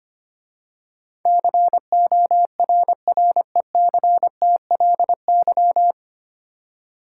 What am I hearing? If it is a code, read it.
Morse code, heard as a keyed tone, about 25 wpm, 700 Hz. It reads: CORRECTLY
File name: practice / 25wpm / 9 Letter Words - First 3 Letters Given - 25wpm